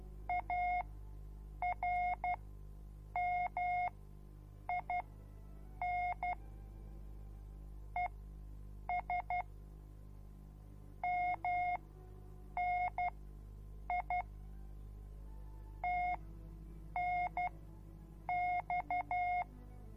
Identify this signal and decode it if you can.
Morse code, heard as a keyed tone, about 12 wpm, spaced as for 7 wpm, 720 Hz. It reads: ARMIN ES MNI TNX